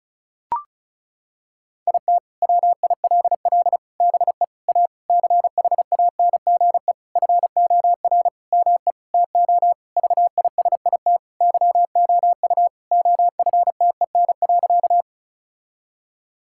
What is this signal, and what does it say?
Morse code, heard as a keyed tone, about 34 wpm, 700 Hz. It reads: IT WILL BE A CHANGE FOR ME TO VISIT YOU OFTEN.
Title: Sentences from Top 1000 Words - Review Part 6 - 35wpm